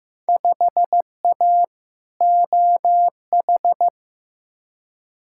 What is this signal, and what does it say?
Morse code, heard as a keyed tone, about 15 wpm, 700 Hz. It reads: 5A OH
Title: Field Day Exchanges Encore - 15wpm